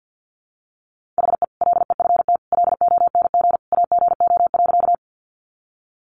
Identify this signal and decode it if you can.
Morse code, heard as a keyed tone, about 50 wpm, 700 Hz. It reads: HE LEFT LONG AGO.
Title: Sentences from Top 500 Words 50wpm